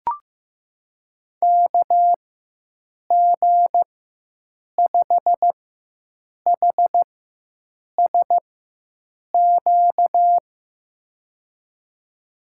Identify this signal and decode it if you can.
Morse code, heard as a keyed tone, about 15 wpm, spaced as for 7 wpm, 700 Hz. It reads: KG5HSQ
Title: USA Call Signs - 6 Characters - 4x Character Spacing - 15wpm